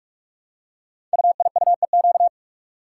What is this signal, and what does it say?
Morse code, heard as a keyed tone, about 44 wpm, 700 Hz. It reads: UIUEQ
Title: Code Groups - 5 Letters - 45wpm